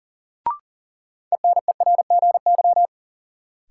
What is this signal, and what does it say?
Morse code, heard as a keyed tone, about 40 wpm, 700 Hz. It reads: ENERGY
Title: Sentences from Top 900 Words - New Word Drill - 40wpm